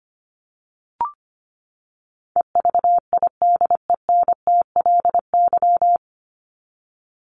Morse code, heard as a keyed tone, about 25 wpm, 700 Hz. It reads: EVIDENTLY
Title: LY Suffix - Top 100 Words with Base Words 25wpm